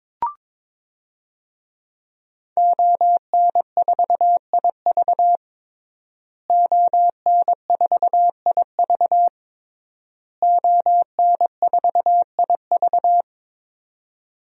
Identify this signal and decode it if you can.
Morse code, heard as a keyed tone, about 22 wpm, 700 Hz. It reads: ON4IV ON4IV ON4IV
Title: DXCC-100 Call Signs - 5 Characters - Repeated 3x - 22wpm